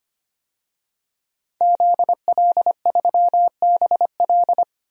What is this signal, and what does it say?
Morse code, heard as a keyed tone, about 25 wpm, 700 Hz. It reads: ZL3BL
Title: DXCC-100 Call Signs - 5 Characters - 25wpm